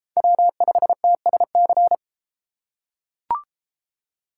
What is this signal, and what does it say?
Morse code, heard as a keyed tone, about 33 wpm, 700 Hz. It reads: W5TSC